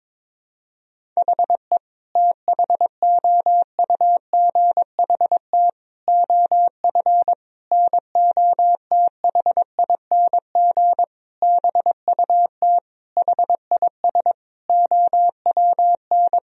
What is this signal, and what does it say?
Morse code, heard as a keyed tone, about 22 wpm, 700 Hz. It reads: HE THOUGHT OF NOTHING BUT HIS OWN
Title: Sentences from Top 1000 Words - Review Part 4 - 22wpm